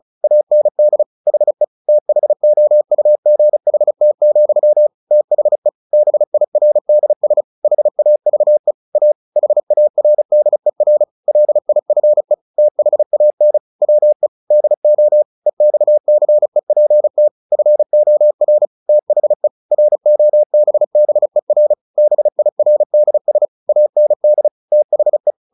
Morse code, heard as a keyed tone, about 34 wpm, 600 Hz. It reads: AND HE THOUGHT, THE BIRDS HAVE A HARDER LIFE THAN WE DO EXCEPT FOR THE ROBBER BIRDS AND THE
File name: the old man and the sea / the old man and the sea 1 35wpm